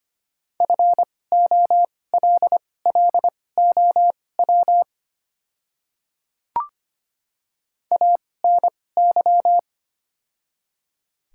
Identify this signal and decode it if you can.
Morse code, heard as a keyed tone, about 25 wpm, spaced as for 18 wpm, 700 Hz. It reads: FOLLOW ANY
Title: 2x Character Spacing - Top 300 Words - 25wpm